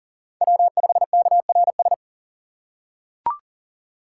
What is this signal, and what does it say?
Morse code, heard as a keyed tone, about 40 wpm, 700 Hz. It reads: W5KRS
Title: USA Call Signs - 4x Character Spacing - 40wpm